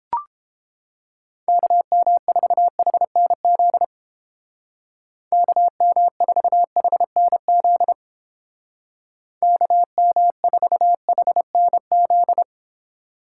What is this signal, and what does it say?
Morse code, heard as a keyed tone, about 30 wpm, 700 Hz. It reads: KM4HNZ KM4HNZ KM4HNZ